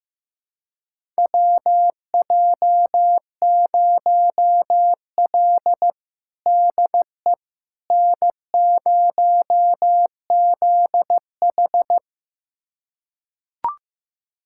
Morse code, heard as a keyed tone, about 16 wpm, 700 Hz. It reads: WJ0L DE N0ZH
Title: USA Callsign DE Callsign - 4 Characters - 15wpm